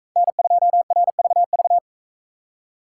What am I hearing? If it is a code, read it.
Morse code, heard as a keyed tone, about 37 wpm, 700 Hz. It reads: N2RVV